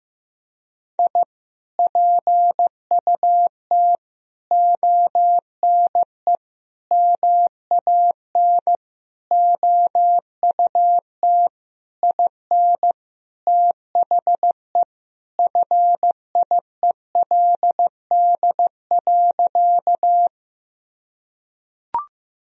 Morse code, heard as a keyed tone, about 16 wpm, 700 Hz. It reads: I PUT ONE MAN OUT IN THE FIELD.
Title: Sentences from Top 1000 Words - Review Part 2 - 15wpm